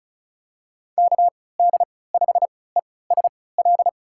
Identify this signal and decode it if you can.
Morse code, heard as a keyed tone, about 36 wpm, spaced as for 20 wpm, 700 Hz. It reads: KD5ESL
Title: USA Call Signs - 6 Characters - 3x Character Spacing - 35wpm